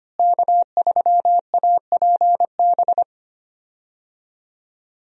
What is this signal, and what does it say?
Morse code, heard as a keyed tone, about 25 wpm, 700 Hz. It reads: K3APB
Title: USA Call Signs - 5 Characters - Repeated 2x - 25wpm